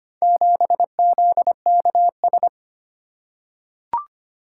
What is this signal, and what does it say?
Morse code, heard as a keyed tone, about 25 wpm, 700 Hz. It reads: 7ZKS
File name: practice / 25wpm / Four Letter-Number Combinations - 25wpm